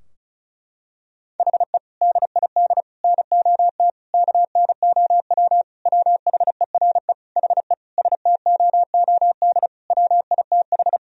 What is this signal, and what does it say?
Morse code, heard as a keyed tone, about 36 wpm, 700 Hz. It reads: HE DID NOT KNOW WHERE HE STOOD WITH
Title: Sentences from Top 1000 Words - Review Part 2 - 35wpm